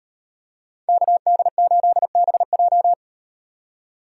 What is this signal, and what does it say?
Morse code, heard as a keyed tone, about 37 wpm, 700 Hz. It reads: KD8BJ